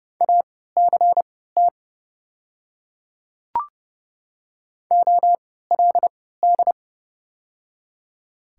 Morse code, heard as a keyed tone, about 30 wpm, spaced as for 17 wpm, 700 Hz. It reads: ACT OLD
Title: 3x Character Spacing - Top 200 Words - 30wpm